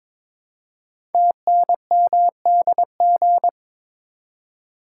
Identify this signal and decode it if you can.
Morse code, heard as a keyed tone, about 22 wpm, 700 Hz. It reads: TNMDG
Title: Code Groups - 5 Letters - 22wpm